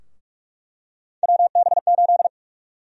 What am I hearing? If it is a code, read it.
Morse code, heard as a keyed tone, about 43 wpm, 700 Hz. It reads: WB8
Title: USA Call Sign Prefixes - 45wpm